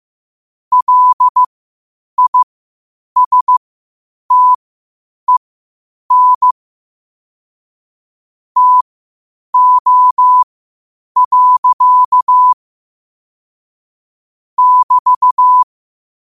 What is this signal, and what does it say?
Morse code, heard as a keyed tone, about 15 wpm, spaced as for 8 wpm, 1000 Hz. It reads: LISTEN TO. <BT>
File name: qotdfiles / 8wpm